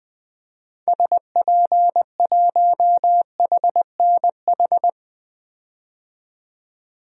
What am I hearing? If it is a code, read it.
Morse code, heard as a keyed tone, about 20 wpm, 700 Hz. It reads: SP1HNH
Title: DXCC-100 Call Signs - 6 Characters - 20wpm